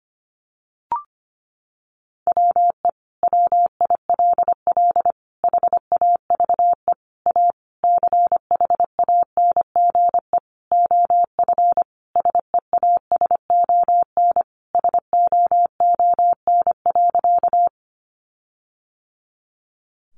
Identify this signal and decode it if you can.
Morse code, heard as a keyed tone, about 25 wpm, 700 Hz. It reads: WE WILL HAVE A CHANGE OF SEASON SOON.
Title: Sentences from Top 1400 Words 25wpm